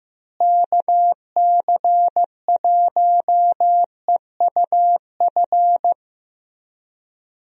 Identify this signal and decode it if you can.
Morse code, heard as a keyed tone, about 15 wpm, 700 Hz. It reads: KC1EUF
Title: USA Call Signs - 6 Characters - Repeated 2x - 15wpm